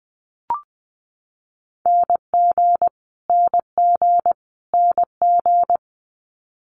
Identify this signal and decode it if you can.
Morse code, heard as a keyed tone, about 20 wpm, 700 Hz. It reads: NG NG NG
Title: Multi-letter Phonemes Repeated 3x 20wpm